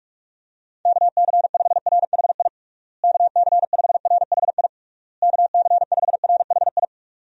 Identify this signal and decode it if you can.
Morse code, heard as a keyed tone, about 41 wpm, 700 Hz. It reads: KC5RHI KC5RHI KC5RHI
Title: USA Call Signs - 6 Characters - Repeated 3x - 45wpm